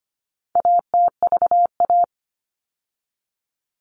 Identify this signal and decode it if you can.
Morse code, heard as a keyed tone, about 25 wpm, 700 Hz. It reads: ATVA
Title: All Cut Numbers - 4-Digit 25wpm